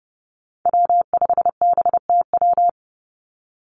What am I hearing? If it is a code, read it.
Morse code, heard as a keyed tone, about 30 wpm, 700 Hz. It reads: W5BTW
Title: USA Call Signs 30wpm